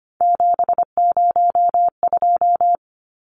Morse code, heard as a keyed tone, about 25 wpm, 690 Hz. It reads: 702
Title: Numbers - 3-Digit 25wpm